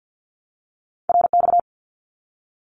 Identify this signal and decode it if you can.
Morse code, heard as a keyed tone, about 48 wpm, 700 Hz. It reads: RX
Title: Abbreviations Repeated 3x - 50wpm